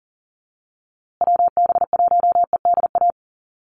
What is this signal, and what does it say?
Morse code, heard as a keyed tone, about 40 wpm, 700 Hz. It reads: WB1EDA
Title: USA Call Signs Repeated 2x 40wpm